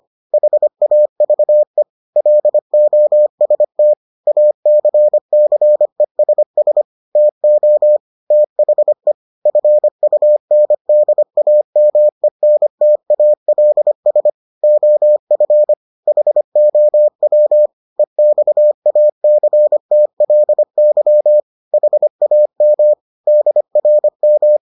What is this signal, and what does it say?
Morse code, heard as a keyed tone, about 25 wpm, 600 Hz. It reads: HAVE LOST ACCESS TO THE FUNDAMENTALS OF HOW EXACTLY HAM DRM